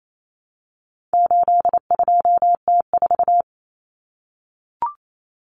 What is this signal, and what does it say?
Morse code, heard as a keyed tone, about 28 wpm, 700 Hz. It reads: 82T4
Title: Common Cut Numbers - 4-Digit 28wpm